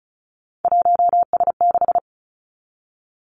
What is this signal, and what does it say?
Morse code, heard as a keyed tone, about 36 wpm, 700 Hz. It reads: 1S6
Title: Three Challenging Letter-Number Combinations 35wpm